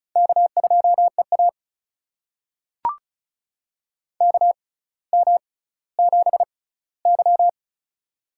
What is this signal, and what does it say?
Morse code, heard as a keyed tone, about 35 wpm, spaced as for 18 wpm, 700 Hz. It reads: K2EA KM7Y
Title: USA Call Signs - 4 Characters - 6x Character Spacing - 35wpm